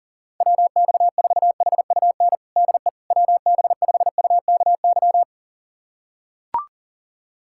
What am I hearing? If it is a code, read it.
Morse code, heard as a keyed tone, about 37 wpm, 700 Hz. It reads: WX4HUN DE WB5UKY
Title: USA Callsign DE Callsign - 6 Characters - 40wpm